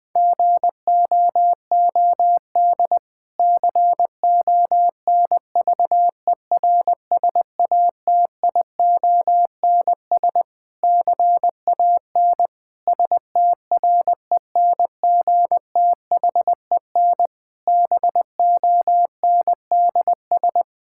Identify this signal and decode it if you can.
Morse code, heard as a keyed tone, about 20 wpm, 700 Hz. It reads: GOOD CONVERSATIONS CAN STRENGTHEN BONDS
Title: Difficult Sentences with Hints - 20wpm